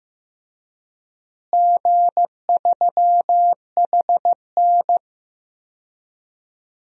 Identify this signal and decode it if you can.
Morse code, heard as a keyed tone, about 15 wpm, 700 Hz. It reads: G3HN
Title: DXCC-100 Call Signs - 4 Characters - Repeated 3x - 15wpm